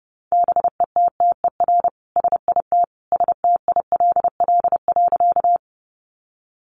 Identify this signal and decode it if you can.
Morse code, heard as a keyed tone, about 28 wpm, 700 Hz. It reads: BETTER SIT STILL.
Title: Sentences from Top 500 Words 30wpm